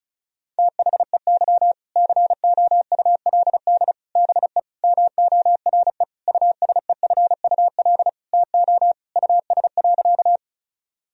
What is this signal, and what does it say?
Morse code, heard as a keyed tone, about 36 wpm, 700 Hz. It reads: THEY COULD BE MORE USEFUL TO US.
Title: Sentences from Top 1000 Words - Review Part 4 - 35wpm